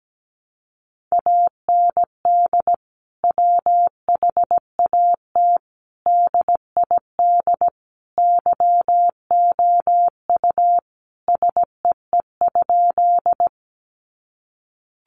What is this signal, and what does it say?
Morse code, heard as a keyed tone, about 17 wpm, 700 Hz. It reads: AND WHAT DID YOU SEE?
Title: Sentences from Top 100 Words 17wpm